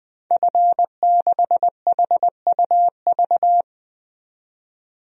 Morse code, heard as a keyed tone, about 20 wpm, 700 Hz. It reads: F6HUV
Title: DXCC-100 Call Signs - 5 Characters - 20wpm